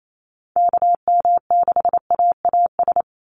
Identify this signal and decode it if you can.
Morse code, heard as a keyed tone, about 28 wpm, 700 Hz. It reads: KM6AAS
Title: USA Call Signs 28wpm